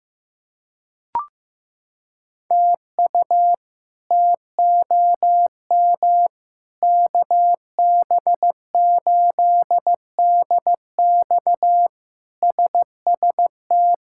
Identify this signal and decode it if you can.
Morse code, heard as a keyed tone, about 16 wpm, 700 Hz. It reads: TU TOM KB8DX SST
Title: K1USN SST QSO Practice - 15wpm